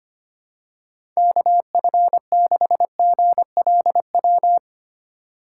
Morse code, heard as a keyed tone, about 25 wpm, 700 Hz. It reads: KF6GLW
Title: USA Call Signs - 6 Characters - 3x Character Spacing - 25wpm